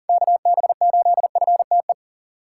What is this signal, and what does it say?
Morse code, heard as a keyed tone, about 40 wpm, 700 Hz. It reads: KB8FTE
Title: USA Call Signs - 6 Characters - 3x Character Spacing - 40wpm